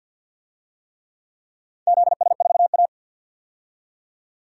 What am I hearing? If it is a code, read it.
Morse code, heard as a keyed tone, about 46 wpm, 700 Hz. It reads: ZS4A
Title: DXCC-100 Call Signs - 4 Characters - 50wpm